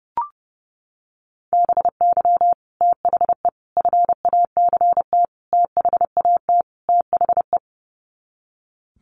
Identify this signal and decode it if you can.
Morse code, heard as a keyed tone, about 30 wpm, 700 Hz. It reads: BY THE FACT THAT THE
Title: Sets of 5 Words Encore 30wpm